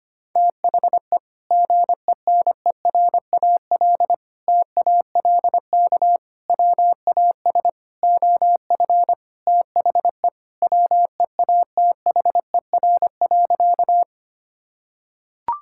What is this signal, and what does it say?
Morse code, heard as a keyed tone, about 25 wpm, 700 Hz. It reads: THE GENERAL TALK WAS OF THE WEATHER.
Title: Sentences from Top 1000 Words - Review Part 5 - 25wpm